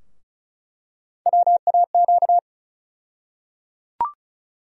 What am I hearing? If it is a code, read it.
Morse code, heard as a keyed tone, about 36 wpm, 700 Hz. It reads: WAQ
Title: USA Call Sign Postfixes - 3 Letters 35wpm